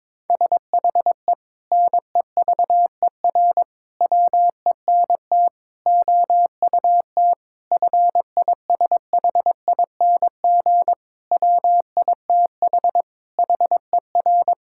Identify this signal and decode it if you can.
Morse code, heard as a keyed tone, about 22 wpm, 700 Hz. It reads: SHE NEVER WENT OUT FISHING WITH HER
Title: Sentences from Top 1000 Words - Review Part 5 - 22wpm